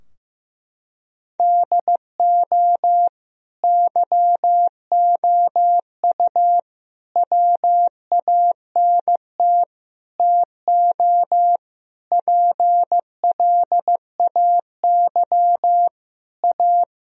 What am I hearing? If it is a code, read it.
Morse code, heard as a keyed tone, about 15 wpm, 700 Hz. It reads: DO YOU WANT TO PLAY A
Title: Sentences from Top 1000 Words - Review Part 3 - 15wpm